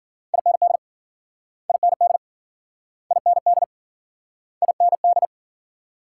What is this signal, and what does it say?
Morse code, heard as a keyed tone, about 52 wpm, 700 Hz. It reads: IND IND IND IND